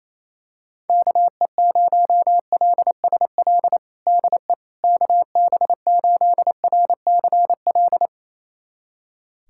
Morse code, heard as a keyed tone, about 28 wpm, 700 Hz. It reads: KE0LSL DE KB8RCL
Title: USA Callsign DE Callsign 28wpm